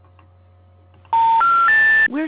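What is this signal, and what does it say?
Special information tone, with speech.
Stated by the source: Outbound phone call reaching an automatic voicemail system and a disconnected-number intercept message.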